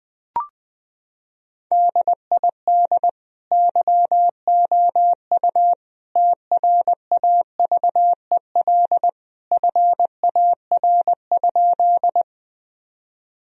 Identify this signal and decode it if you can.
Morse code, heard as a keyed tone, about 20 wpm, 700 Hz. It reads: DID YOU TRAVEL FAR?
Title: Sentences from Top 1000 Words - Review Part 2 - 20wpm